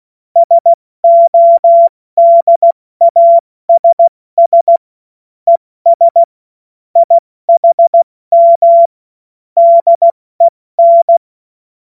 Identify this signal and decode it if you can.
Morse code, heard as a keyed tone, about 16 wpm, 680 Hz. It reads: SODASS ES IHM DEN
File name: Morsekurs / Witze / Witz 19 Dezember 2023 70 F80